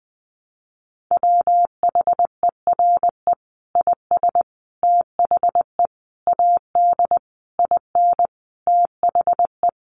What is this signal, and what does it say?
Morse code, heard as a keyed tone, about 20 wpm, 700 Hz. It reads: WHERE IS THE AD IN THE
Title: Sentences from Top 1300 Words 20wpm